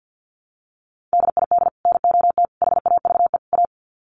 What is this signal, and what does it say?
Morse code, heard as a keyed tone, about 50 wpm, 700 Hz. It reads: DID NOT HAVE A
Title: Sets of 4 Words 50wpm